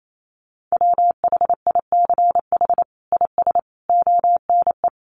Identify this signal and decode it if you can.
Morse code, heard as a keyed tone, about 28 wpm, 700 Hz. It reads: WHICH IS ONE
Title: Sets of 5 Words Encore 28wpm